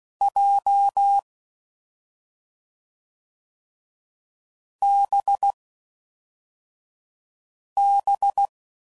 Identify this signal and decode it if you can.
Morse code, heard as a keyed tone, about 16 wpm, 800 Hz. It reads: J B B